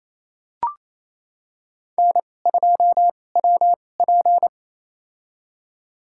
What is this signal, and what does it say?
Morse code, heard as a keyed tone, about 28 wpm, spaced as for 20 wpm, 690 Hz. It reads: N2WP
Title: USA Call Signs - 2x Character Spacing - 28wpm